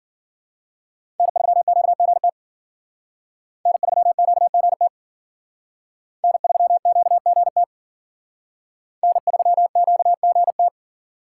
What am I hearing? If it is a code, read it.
Morse code, heard as a keyed tone, about 48 wpm, 700 Hz. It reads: N3QGT N3QGT N3QGT N3QGT
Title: Speed Racing - USA 1x3 Call Signs 40wpm